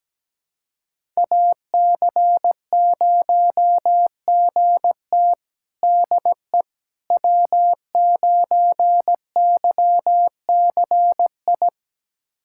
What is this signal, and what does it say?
Morse code, heard as a keyed tone, about 17 wpm, 700 Hz. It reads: AC0GT DE W9YCI
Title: USA Callsign DE Callsign - 5 Characters - 17wpm